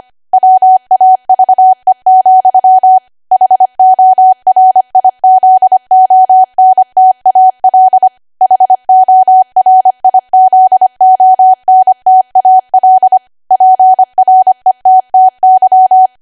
Morse code, heard as a keyed tone, about 25 wpm, 750 Hz. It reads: WAVE, HORIZONTAL HORIZONTAL PRETTY
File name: Archive / 25 WPM / 25WPM